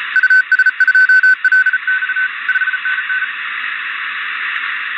Morse code, heard as a keyed tone, about 34 wpm, 1500 Hz, with noise. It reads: US2LX 5NN